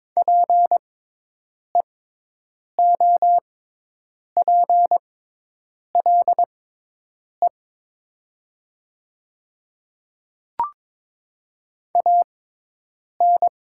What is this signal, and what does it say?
Morse code, heard as a keyed tone, about 22 wpm, spaced as for 8 wpm, 700 Hz. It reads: PEOPLE AN
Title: 6x Character Spacing - Top 100 Words - 22wpm